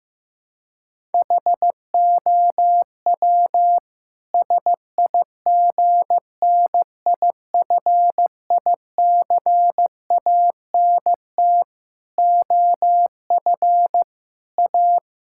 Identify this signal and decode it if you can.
Morse code, heard as a keyed tone, about 16 wpm, 700 Hz. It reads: HOW SIGNIFICANT OF A
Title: Sentences from Top 1000 Words - Review Part 6 - 15wpm